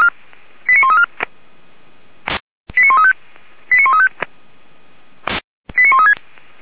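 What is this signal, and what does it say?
Five-tone selective call (ZVEI), heard as quick runs of five tones, tones of about 70 ms. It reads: 89134 89135 89135 89136
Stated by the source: ZVEI five-tone selective call 89133, received by SDR, from the audio archive of Feuerwehr Orlamünde.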